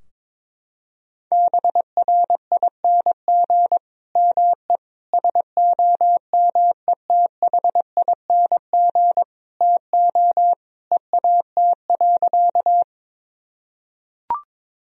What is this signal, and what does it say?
Morse code, heard as a keyed tone, about 22 wpm, 700 Hz. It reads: BRING ME SOMETHING TO EAT.
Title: Sentences from Top 1000 Words - Review Part 3 - 22wpm